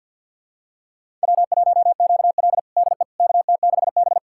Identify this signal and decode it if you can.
Morse code, heard as a keyed tone, about 46 wpm, 700 Hz. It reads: W1QL DE KT6B
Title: USA Callsign DE Callsign - 4 Characters - 50wpm